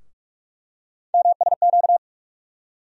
Morse code, heard as a keyed tone, about 45 wpm, 700 Hz. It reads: MSQ
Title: USA Call Sign Postfixes - 3 Letters 45wpm